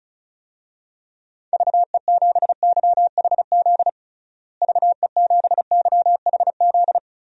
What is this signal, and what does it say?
Morse code, heard as a keyed tone, about 36 wpm, 700 Hz. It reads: VE7YHZ VE7YHZ
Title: DXCC-100 Call Signs - 6 Characters - Repeated 3x - 35wpm